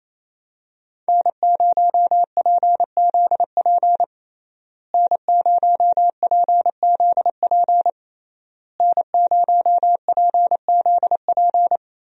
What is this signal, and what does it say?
Morse code, heard as a keyed tone, about 28 wpm, 700 Hz. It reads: N0PZP N0PZP N0PZP
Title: USA Call Signs - 5 Characters - Repeated 3x - 28wpm